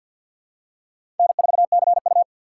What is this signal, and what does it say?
Morse code, heard as a keyed tone, about 52 wpm, 700 Hz. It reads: N4CU
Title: USA Call Signs - 4 Characters - 3x Character Spacing - 50wpm